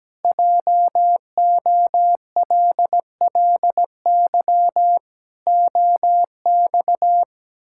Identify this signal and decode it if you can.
Morse code, heard as a keyed tone, about 17 wpm, 700 Hz. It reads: JOLLY OX
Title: Pangram Challenge v2 - 17wpm